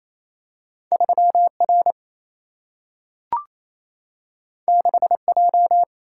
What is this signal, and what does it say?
Morse code, heard as a keyed tone, about 28 wpm, 700 Hz. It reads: 3 R 6 J
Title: Single-Letter Single-Number 28wpm